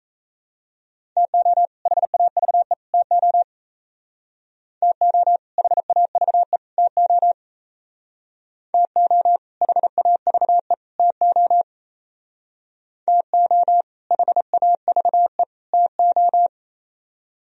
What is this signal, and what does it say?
Morse code, heard as a keyed tone, about 34 wpm, 700 Hz. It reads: TO HAVE TO TO HAVE TO TO HAVE TO TO HAVE TO
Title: Speed-Racing - Sets of 3 Words 28wpm